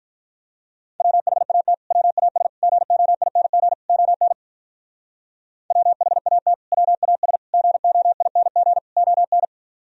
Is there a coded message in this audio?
Yes: Morse code from 1.0 to 9.4 s, heard as a keyed tone, about 47 wpm, 700 Hz. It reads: WHAT WAS GOING ON WHAT WAS GOING ON